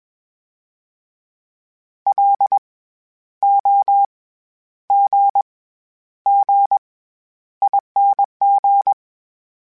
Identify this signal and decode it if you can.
Morse code, heard as a keyed tone, about 20 wpm, 800 Hz. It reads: L O G G ING
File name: Special INGs-mp3s / Special INGs-mp3s / SlowToFaster ing @20